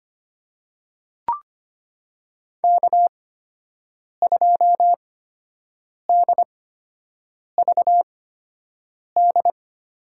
Morse code, heard as a keyed tone, about 25 wpm, spaced as for 7 wpm, 700 Hz. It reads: K2DVD
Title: USA Call Signs - 5 Characters - 8x Character Spacing - 25wpm